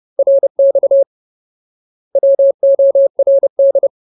Morse code, heard as a keyed tone, about 30 wpm, 550 Hz. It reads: RX WORD
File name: Ultimate Word List / 30WPM Ultimate